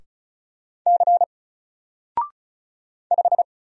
Morse code, heard as a keyed tone, about 35 wpm, 700 Hz. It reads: C 5
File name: Single-Letter-Number-35wpm